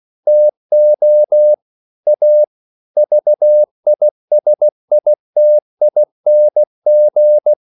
Morse code, heard as a keyed tone, about 16 wpm, 600 Hz. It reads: TO A VISITING
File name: CW / Practice / EnglishQuotations0116